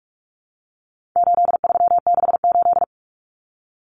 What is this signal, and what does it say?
Morse code, heard as a keyed tone, about 44 wpm, 700 Hz. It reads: 8368
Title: Numbers - 4-Digit 45wpm